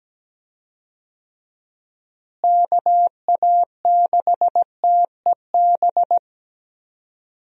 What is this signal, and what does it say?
Morse code, heard as a keyed tone, about 17 wpm, 700 Hz. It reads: KA6TEB
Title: USA Call Signs - 6 Characters - Repeated 3x - 17wpm